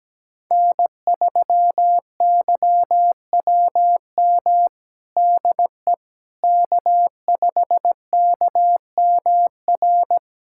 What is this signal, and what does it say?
Morse code, heard as a keyed tone, about 17 wpm, 700 Hz. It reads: N3YWM DE K5KMR
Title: USA Callsign DE Callsign - 5 Characters - 17wpm